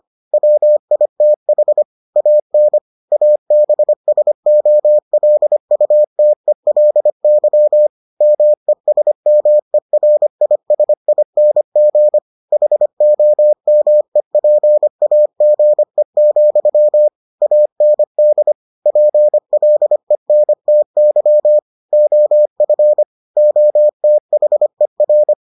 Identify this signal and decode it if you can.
Morse code, heard as a keyed tone, about 25 wpm, 600 Hz. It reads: WITH AN ABSOLUTELY MESMERISING HOMEPAGE, AND PLENTY OF OTHER